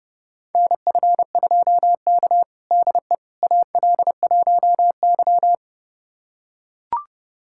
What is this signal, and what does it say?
Morse code, heard as a keyed tone, about 30 wpm, 700 Hz. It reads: NF2K DE AL1Y
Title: USA Callsign DE Callsign - 4 Characters - 30wpm